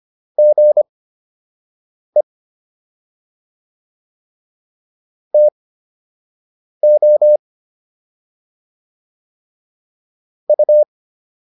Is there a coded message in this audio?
Yes: Morse code from 0.4 to 10.8 s, heard as a keyed tone, about 25 wpm, spaced as for 6 wpm, 600 Hz. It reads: GE TO U